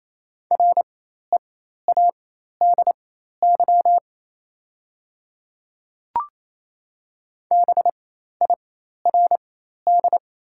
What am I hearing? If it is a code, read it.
Morse code, heard as a keyed tone, about 28 wpm, spaced as for 13 wpm, 700 Hz. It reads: READY BIRD
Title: 4x Character Spacing - Top 500 Words - 28wpm